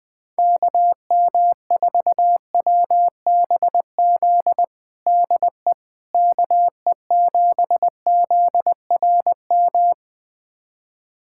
Morse code, heard as a keyed tone, about 20 wpm, 700 Hz. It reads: KM4WBZ DE KE7ZRM
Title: USA Callsign DE Callsign 20wpm